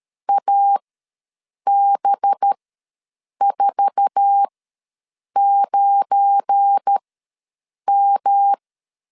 Morse code, heard as a keyed tone, about 13 wpm, spaced as for 7 wpm, 800 Hz. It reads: AB49M
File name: LEZIONI / 9 2 ws 12wpm 3406